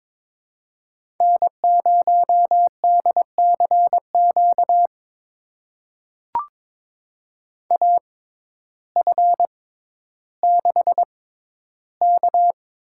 Morse code, heard as a keyed tone, about 22 wpm, spaced as for 12 wpm, 700 Hz. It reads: N0DCQ AF6K
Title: USA Call Signs - 6x Character Spacing - 22wpm